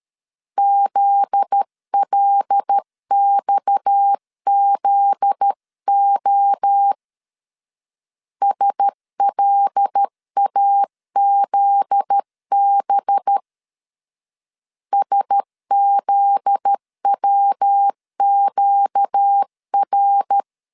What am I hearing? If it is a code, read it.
Morse code, heard as a keyed tone, about 13 wpm, 800 Hz. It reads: ZLXZO SLAZB SZWQR
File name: LEZIONI / 7 GF9Z / 7 GF9Z 12wpm 2147